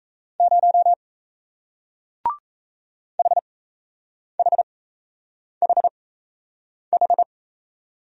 Morse code, heard as a keyed tone, about 37 wpm, spaced as for 8 wpm, 700 Hz. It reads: 0 HHHH